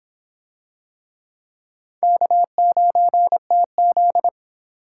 Morse code, heard as a keyed tone, about 25 wpm, 700 Hz. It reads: K9TZ